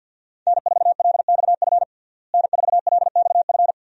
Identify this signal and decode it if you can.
Morse code, heard as a keyed tone, about 53 wpm, 700 Hz. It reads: N4LXF N4LXF
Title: USA Call Signs - 5 Characters - Repeated 2x - 50wpm